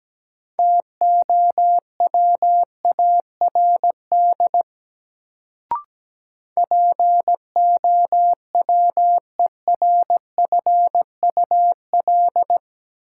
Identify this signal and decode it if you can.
Morse code, heard as a keyed tone, about 17 wpm, 700 Hz. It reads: TOWARD POWERFUL
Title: Sentences from Top 1300 Words - New Word Drill Alt - 17wpm